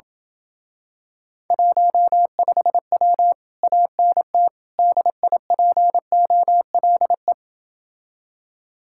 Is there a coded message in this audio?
Yes: Morse code from 1.5 to 7.3 s, heard as a keyed tone, about 28 wpm, 700 Hz. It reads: 15W ANT DIPOLE